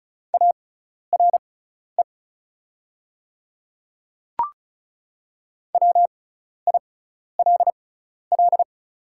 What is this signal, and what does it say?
Morse code, heard as a keyed tone, about 34 wpm, spaced as for 12 wpm, 690 Hz. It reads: ARE WILL
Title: 6x Character Spacing - Top 300 Words - 35wpm